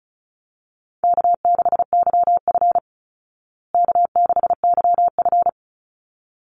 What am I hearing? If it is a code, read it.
Morse code, heard as a keyed tone, about 34 wpm, 700 Hz. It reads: K6YF K6YF
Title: Easy Call Signs Repeated 2x - 35wpm